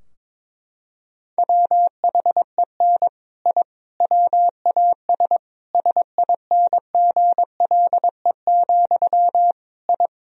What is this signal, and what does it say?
Morse code, heard as a keyed tone, about 22 wpm, 700 Hz. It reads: WHEN I WAS SINGLE, I
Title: Sentences from Top 1000 Words - Review Part 4 - 22wpm